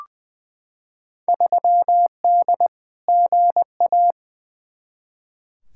Morse code, heard as a keyed tone, about 20 wpm, 700 Hz. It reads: 3D GA
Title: Field Day Exchanges Encore - 20wpm